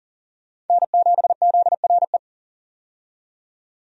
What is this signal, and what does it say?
Morse code, heard as a keyed tone, about 40 wpm, 700 Hz. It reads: N7ZRE
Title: USA Call Signs - 5 Characters - 40wpm